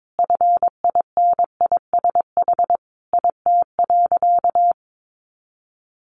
Morse code, heard as a keyed tone, about 22 wpm, 700 Hz. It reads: FINISH IT.
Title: Additional Easy Phrases 22wpm